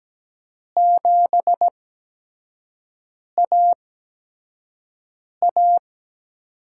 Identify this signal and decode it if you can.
Morse code, heard as a keyed tone, about 17 wpm, spaced as for 5 wpm, 700 Hz. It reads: 7AA
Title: USA Call Signs - 4 Characters - 8x Character Spacing - 17wpm